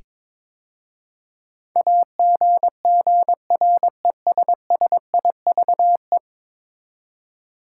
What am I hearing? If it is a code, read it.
Morse code, heard as a keyed tone, about 21 wpm, 700 Hz. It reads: AGGRESSIVE
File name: practice / 22wpm / 10 Letter Words - First 3 Letters Given - 22wpm